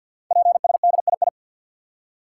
Morse code, heard as a keyed tone, about 48 wpm, 700 Hz. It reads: PSDII
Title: Code Groups - 5 Letter:Numbers - 50wpm